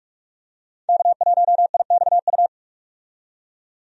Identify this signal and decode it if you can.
Morse code, heard as a keyed tone, about 43 wpm, 700 Hz. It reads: K1IXU
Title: USA Call Signs - 5 Characters - 45wpm